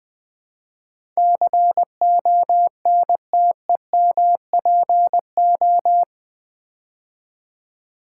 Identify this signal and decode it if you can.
Morse code, heard as a keyed tone, about 20 wpm, 700 Hz. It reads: CONTEMPO
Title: Build A Word - 4 Letters At A Time - 12-Letter Words - 20wpm